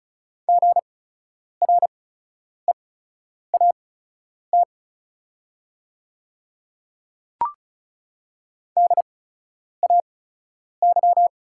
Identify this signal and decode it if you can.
Morse code, heard as a keyed tone, about 36 wpm, spaced as for 10 wpm, 690 Hz. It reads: GREAT DAY
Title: 8x Character Spacing - Top 200 Words - 35wpm